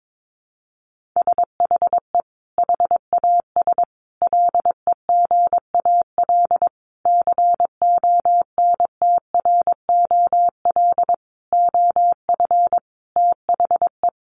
Morse code, heard as a keyed tone, about 22 wpm, 700 Hz. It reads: SHE HAS LEGAL CONTROL OF THE
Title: Sentences from Top 1200 Words 22wpm